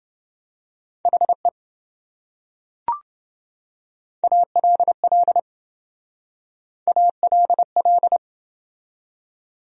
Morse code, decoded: HE ALL ALL

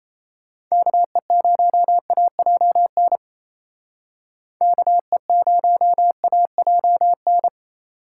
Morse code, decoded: KE0AJN KE0AJN

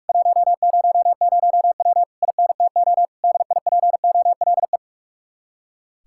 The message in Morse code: 100W INTO DIPOLE

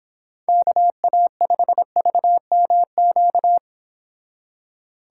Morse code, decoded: KA5VMQ

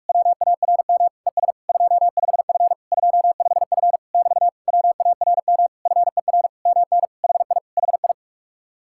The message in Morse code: WARM ES 25F 25F <BT> WARM FER MN HI HI

